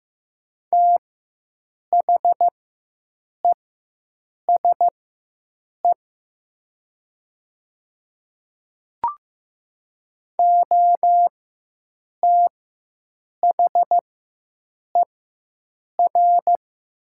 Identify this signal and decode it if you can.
Morse code, heard as a keyed tone, about 15 wpm, spaced as for 7 wpm, 700 Hz. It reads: THESE OTHER
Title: 4x Character Spacing - Top 500 Words - 15wpm